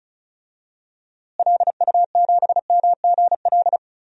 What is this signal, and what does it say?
Morse code, heard as a keyed tone, about 34 wpm, 700 Hz. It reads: LU7MGL
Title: DXCC-100 Call Signs - 6 Characters - Repeated 2x - 35wpm